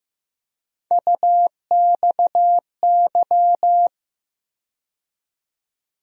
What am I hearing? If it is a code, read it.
Morse code, heard as a keyed tone, about 15 wpm, 700 Hz. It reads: UXY
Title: USA Call Sign Postfixes - 3 Letters 15wpm